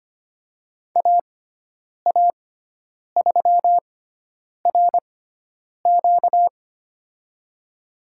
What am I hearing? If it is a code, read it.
Morse code, heard as a keyed tone, about 25 wpm, spaced as for 9 wpm, 700 Hz. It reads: AA3RQ
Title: USA Call Signs - 6x Character Spacing - 25wpm